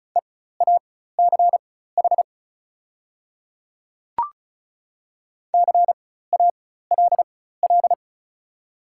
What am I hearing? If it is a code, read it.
Morse code, heard as a keyed tone, about 36 wpm, spaced as for 16 wpm, 700 Hz. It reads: EACH CALL
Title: 4x Character Spacing - Top 100 Words - 35wpm